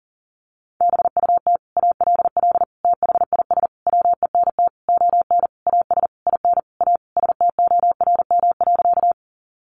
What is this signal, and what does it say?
Morse code, heard as a keyed tone, about 40 wpm, 700 Hz. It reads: BUT ALL THIS WENT ON AS IN A STORM.
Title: Sentences from Top 1500 Words 40wpm